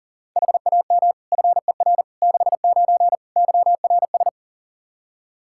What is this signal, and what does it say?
Morse code, heard as a keyed tone, about 40 wpm, 700 Hz. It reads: HAM FER 69 YRS
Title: K1USN SST QSO Practice 40wpm